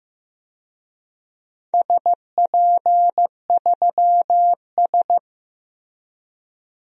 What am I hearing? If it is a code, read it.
Morse code, heard as a keyed tone, about 15 wpm, 700 Hz. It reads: SP3S
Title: DXCC-100 Call Signs - 4 Characters - Repeated 2x - 15wpm